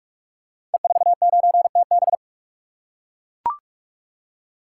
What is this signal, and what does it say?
Morse code, heard as a keyed tone, about 43 wpm, 700 Hz. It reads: E49TB
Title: Code Groups - 5 Letter:Numbers - 45wpm